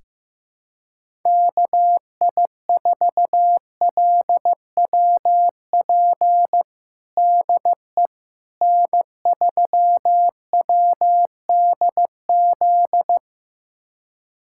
Morse code, heard as a keyed tone, about 15 wpm, 700 Hz. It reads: KI4LWP DE N3WDZ
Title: USA Callsign DE Callsign 15wpm